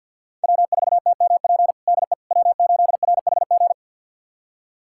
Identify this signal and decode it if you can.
Morse code, heard as a keyed tone, about 52 wpm, 700 Hz. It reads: W4TMP DE W8RHG
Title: USA Callsign DE Callsign - 5 Characters - 50wpm